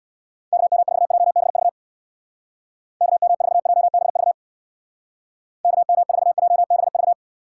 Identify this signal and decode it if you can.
Morse code, heard as a keyed tone, about 62 wpm, 700 Hz. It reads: KM4JBV KM4JBV KM4JBV
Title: Speed Racing - USA 2x3 Call Signs 50wpm